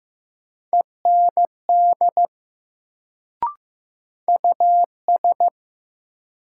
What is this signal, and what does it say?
Morse code, heard as a keyed tone, about 15 wpm, 700 Hz. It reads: END US
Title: Sentences from Top 200 Words - New Word Drill - 15wpm